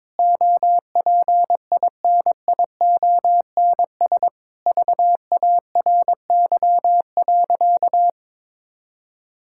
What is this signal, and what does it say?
Morse code, heard as a keyed tone, about 22 wpm, 700 Hz. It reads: OPINIONS VARY.
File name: Sentences with 7-Word Phrases - 22wpm